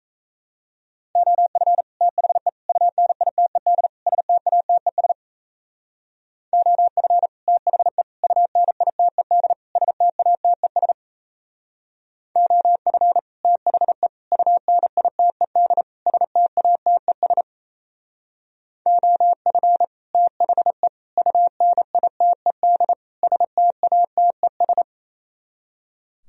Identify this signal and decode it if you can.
Morse code, heard as a keyed tone, about 35 wpm, 700 Hz. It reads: OF THE UNITED STATES OF THE UNITED STATES OF THE UNITED STATES OF THE UNITED STATES